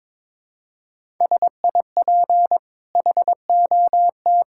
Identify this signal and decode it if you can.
Morse code, heard as a keyed tone, about 22 wpm, 690 Hz. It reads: SIP HOT
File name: practice / 22wpm / Easy Sentences with Hints - 22wpm